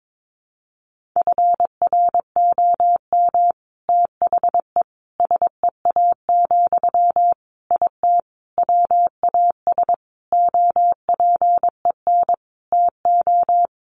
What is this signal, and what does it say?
Morse code, heard as a keyed tone, about 22 wpm, 700 Hz. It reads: FROM THE SEA, IT WAS OPEN TO
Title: Sentences from Top 1500 Words 22wpm